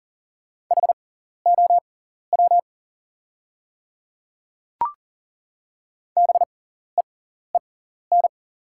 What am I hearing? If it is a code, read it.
Morse code, heard as a keyed tone, about 40 wpm, spaced as for 14 wpm, 700 Hz. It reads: HOW BEEN
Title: 6x Character Spacing - Top 100 Words - 40wpm